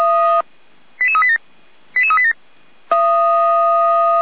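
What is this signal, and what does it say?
Five-tone selective call (ZVEI), heard as quick runs of five tones, tones of about 70 ms. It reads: 88387 88387
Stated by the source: ZVEI five-tone selective call received by SDR.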